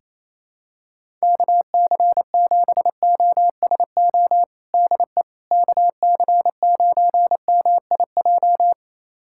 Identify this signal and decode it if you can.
Morse code, heard as a keyed tone, about 28 wpm, 700 Hz. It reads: KC7OSO DE KC9MIJ
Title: USA Callsign DE Callsign - 6 Characters - 28wpm